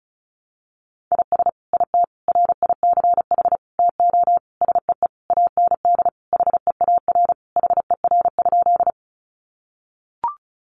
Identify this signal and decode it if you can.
Morse code, heard as a keyed tone, about 34 wpm, 700 Hz. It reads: IS IT RICH TO SEE AND HEAR HER?
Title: Sentences from Top 1100 Words 35wpm